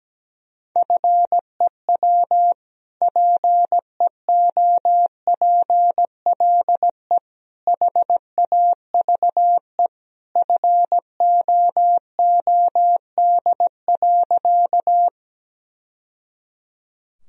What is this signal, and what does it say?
Morse code, heard as a keyed tone, about 17 wpm, 700 Hz. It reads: FEW PEOPLE HAVE FOOD.
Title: Sentences from Top 1000 Words - Review Part 1 - 17wpm